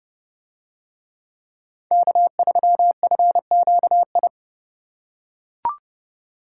Morse code, heard as a keyed tone, about 29 wpm, 700 Hz. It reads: K3FQI